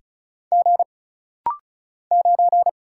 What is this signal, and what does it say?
Morse code, heard as a keyed tone, about 36 wpm, 700 Hz. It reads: G 9
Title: Single Letter-Number - Rapid-Fire 35wpm